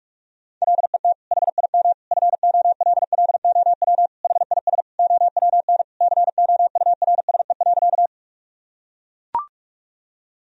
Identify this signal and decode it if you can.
Morse code, heard as a keyed tone, about 43 wpm, 700 Hz. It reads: LET HIM FOLLOW HIS OWN COURSE.
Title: Sentences from Top 1000 Words - Review Part 1 - 45wpm